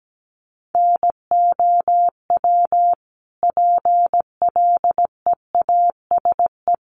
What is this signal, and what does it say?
Morse code, heard as a keyed tone, about 17 wpm, 700 Hz. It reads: NOW PLEASE
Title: Sentences from Top 1500 Words 17wpm